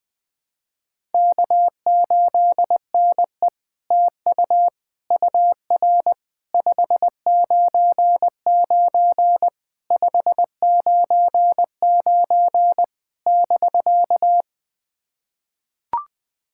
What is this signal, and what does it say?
Morse code, heard as a keyed tone, about 20 wpm, 700 Hz. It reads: K8NE TU UR 599 599 <BK>
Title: SOTA QSO Practice 20wpm